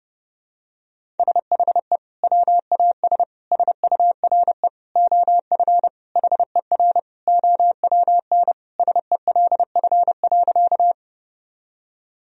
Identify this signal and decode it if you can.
Morse code, heard as a keyed tone, about 30 wpm, 700 Hz. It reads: SHE WAS SURE OF HER OWN SELF.
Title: Sentences from Top 1000 Words - Review Part 1 - 30wpm